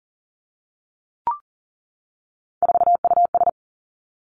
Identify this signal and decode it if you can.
Morse code, heard as a keyed tone, about 38 wpm, 700 Hz. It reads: 4US
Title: Three Challenging Letter-Number Combinations 40wpm